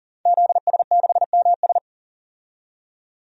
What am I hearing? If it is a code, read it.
Morse code, heard as a keyed tone, about 40 wpm, 700 Hz. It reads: ZS6MS
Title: DXCC-100 Call Signs - 5 Characters - 40wpm